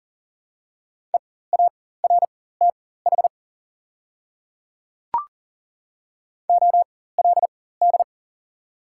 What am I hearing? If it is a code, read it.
Morse code, heard as a keyed tone, about 40 wpm, spaced as for 19 wpm, 690 Hz. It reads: EARTH OLD
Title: 4x Character Spacing - Top 200 Words - 40wpm